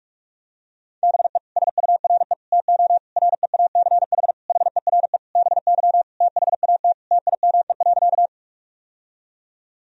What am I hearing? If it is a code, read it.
Morse code, heard as a keyed tone, about 41 wpm, 700 Hz. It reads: BE SURE TO REACH HERE BY THAT TIME.
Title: Sentences from Top 1000 Words - Review Part 2 - 45wpm